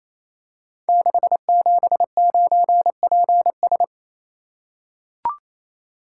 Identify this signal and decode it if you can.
Morse code, heard as a keyed tone, about 28 wpm, 700 Hz. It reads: 679PS
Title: Code Groups - 5 Letter:Numbers - 28wpm